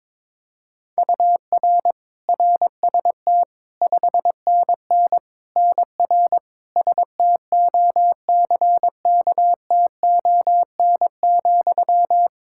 Morse code, heard as a keyed tone, about 22 wpm, 700 Hz. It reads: UR RST 5NN NR STOCKTON,
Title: Practice QSO - Exchange 1 - Beginning - 22wpm